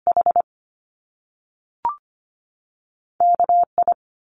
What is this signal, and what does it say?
Morse code, heard as a keyed tone, about 25 wpm, 700 Hz. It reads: H KI
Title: Top Two Letter Combinations in Words 25wpm